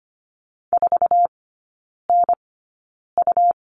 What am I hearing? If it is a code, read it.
Morse code, heard as a keyed tone, about 25 wpm, spaced as for 9 wpm, 700 Hz. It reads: 4NU
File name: Lesson 17e - Letter u - Three Characters 25wpm